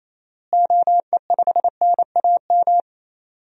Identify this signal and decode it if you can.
Morse code, heard as a keyed tone, about 28 wpm, 700 Hz. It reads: OE5NAM
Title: DXCC-100 Call Signs - 6 Characters - Repeated 2x - 28wpm